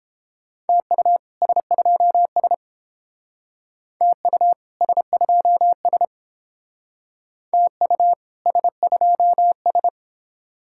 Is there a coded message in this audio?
Yes: Morse code from 0.7 to 9.9 s, heard as a keyed tone, about 28 wpm, 700 Hz. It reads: TU S2S TU S2S TU S2S